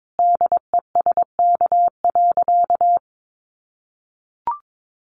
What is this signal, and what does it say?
Morse code, heard as a keyed tone, about 22 wpm, 700 Hz. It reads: DESK.
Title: Additional Easy Phrases 22wpm